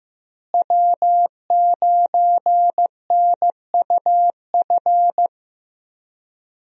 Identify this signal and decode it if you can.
Morse code, heard as a keyed tone, about 16 wpm, 700 Hz. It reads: W9NUF
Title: USA Call Signs - 5 Characters - Repeated 3x - 15wpm